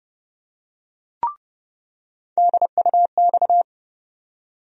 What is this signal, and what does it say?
Morse code, heard as a keyed tone, about 30 wpm, 700 Hz. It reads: DUX
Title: USA Call Sign Postfixes - 3 Letters 30wpm